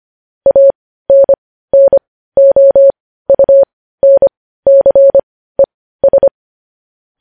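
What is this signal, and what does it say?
Morse code, heard as a keyed tone, about 25 wpm, spaced as for 15 wpm, 550 Hz. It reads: ANNOUNCES
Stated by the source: Morse code news headlines at 15 wpm.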